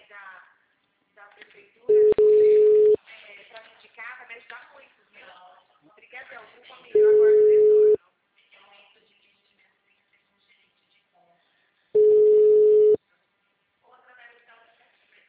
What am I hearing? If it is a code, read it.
Ringback tone (Europe, 425 Hz), with speech.
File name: out- -2018- - - .37959